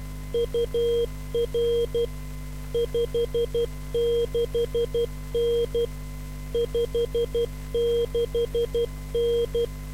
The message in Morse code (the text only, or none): UR 56N 56N